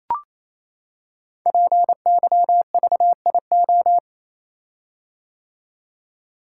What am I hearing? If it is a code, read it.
Morse code, heard as a keyed tone, about 27 wpm, 700 Hz. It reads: PYVIO
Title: Code Groups - 5 Letter:Numbers - 28wpm